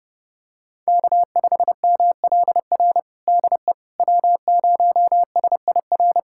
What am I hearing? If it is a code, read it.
Morse code, heard as a keyed tone, about 32 wpm, 700 Hz. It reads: K5MLR DE W0SIR
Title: USA Callsign DE Callsign - 5 Characters - 30wpm